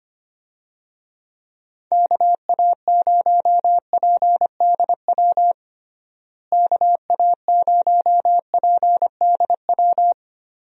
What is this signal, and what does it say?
Morse code, heard as a keyed tone, about 25 wpm, 700 Hz. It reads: KA0PDW KA0PDW
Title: USA Call Signs - 6 Characters - Repeated 3x - 25wpm